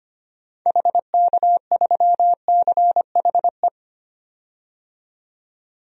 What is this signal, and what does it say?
Morse code, heard as a keyed tone, about 25 wpm, 700 Hz. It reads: HK3CHE
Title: DXCC-100 Call Signs - 6 Characters - 25wpm